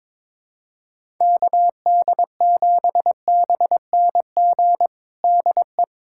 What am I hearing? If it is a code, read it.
Morse code, heard as a keyed tone, about 22 wpm, 700 Hz. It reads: KD7BNG DE
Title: USA Callsign DE Callsign - 6 Characters - 22wpm